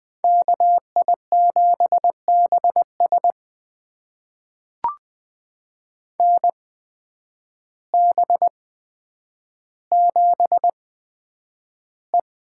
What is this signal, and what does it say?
Morse code, heard as a keyed tone, about 20 wpm, spaced as for 9 wpm, 700 Hz. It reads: KI7BS NB7E
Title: USA Call Signs - 8x Character Spacing - 20wpm